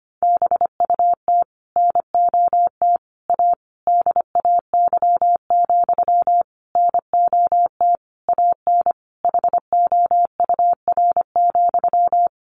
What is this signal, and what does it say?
Morse code, heard as a keyed tone, about 25 wpm, 700 Hz. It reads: BUT NOT A DAY, NOT AN HOUR,
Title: Sentences from Top 1400 Words 25wpm